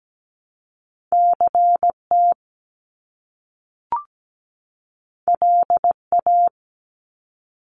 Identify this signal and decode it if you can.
Morse code, heard as a keyed tone, about 17 wpm, 700 Hz. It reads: CT LA
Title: US States 17wpm